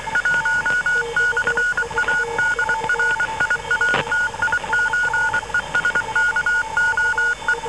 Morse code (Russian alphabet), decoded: 1/АДРИАТИЧЕСКОЕ